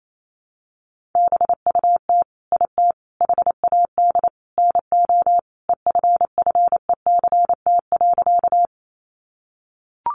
Morse code, heard as a keyed tone, about 28 wpm, 700 Hz. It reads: BUT IT HAD NO EFFECT.
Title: Sentences from Top 1000 Words 28wpm